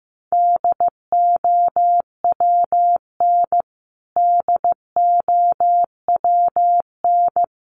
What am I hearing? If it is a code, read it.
Morse code, heard as a keyed tone, about 15 wpm, 700 Hz. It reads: DOWN DOWN
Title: Top 300 Words Repeated 15wpm